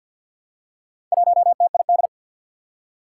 Morse code, heard as a keyed tone, about 52 wpm, 700 Hz. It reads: 1TID
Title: Four Challenging Letter-Number Combinations 50wpm